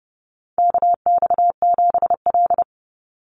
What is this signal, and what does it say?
Morse code, heard as a keyed tone, about 30 wpm, 700 Hz. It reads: KX7L
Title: NA SOTA Chaser Callsigns 30wpm